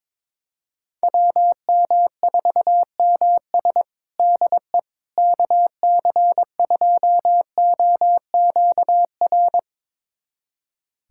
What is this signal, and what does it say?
Morse code, heard as a keyed tone, about 21 wpm, 700 Hz. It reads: WM4MS DE KC2OQR
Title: USA Callsign DE Callsign 22wpm